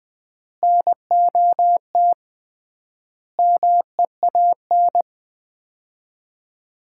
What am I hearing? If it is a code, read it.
Morse code, heard as a keyed tone, about 20 wpm, 700 Hz. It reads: NOT MEAN
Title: 3x Word-Spacing - Sets of 5 Words - 20wpm